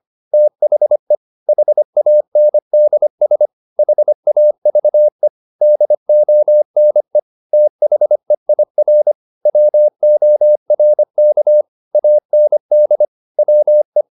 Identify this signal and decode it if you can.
Morse code, heard as a keyed tone, about 25 wpm, 600 Hz. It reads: THE HANDS HAVE DONE THEIR WORK AND WE